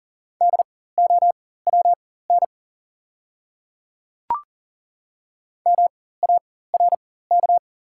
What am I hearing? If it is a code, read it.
Morse code, heard as a keyed tone, about 42 wpm, spaced as for 19 wpm, 690 Hz. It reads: DOWN MARK
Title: 4x Character Spacing - Top 300 Words - 40wpm